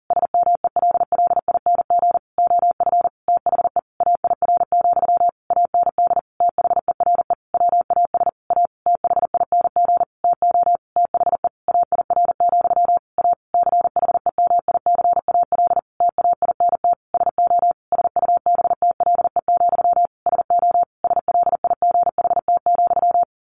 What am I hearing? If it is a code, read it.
Morse code, heard as a keyed tone, about 40 wpm, 700 Hz. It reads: SMELLING OF THE AIR, AND THERE WAS A THING TO THE AIR, A CHEMICAL TAINT SO SUBTLE, SO SLIGHT,
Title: A Sound Of Thunder 40wpm